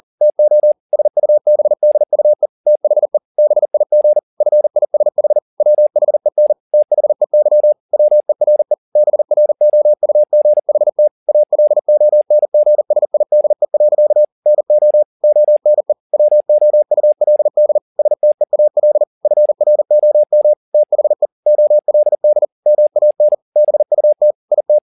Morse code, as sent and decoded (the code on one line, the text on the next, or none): - --- / ... ..- -... -.. ..- . / - .... / -... .. --. / ..-. .. ... .... / .-- .... -. / - .... -.-- / .-- . .-. / -... .-. --- ..- --. .... - / .- .-.. --- -. --. ... .. -.. .-.-.- / -. --- / --- -. / .-- --- ..- .-.. -.. / ... - . .- .-.. / ..-. .-. --- -- / - .... / --- .-.. -.. / -- .- -. / -... ..- - / .. -
TO SUBDUE THE BIG FISH WHEN THEY WERE BROUGHT ALONGSIDE. NO ONE WOULD STEAL FROM THE OLD MAN BUT IT